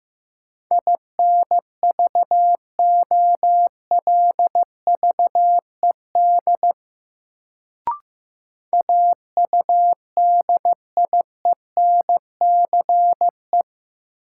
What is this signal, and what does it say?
Morse code, heard as a keyed tone, about 15 wpm, 700 Hz. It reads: INVOLVED AUDIENCE
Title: Sentences from Top 800 Words - New Word Drill - 15wpm